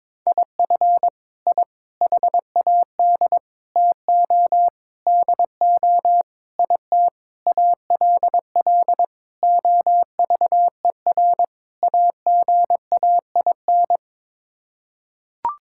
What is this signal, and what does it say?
Morse code, heard as a keyed tone, about 22 wpm, 700 Hz. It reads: IF I HAD TO DO IT ALL OVER AGAIN
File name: Sets of 9 Words - 22wpm